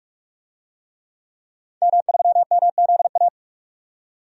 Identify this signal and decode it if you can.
Morse code, heard as a keyed tone, about 46 wpm, 700 Hz. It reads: M3MZA
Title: DXCC-100 Call Signs - 5 Characters - 45wpm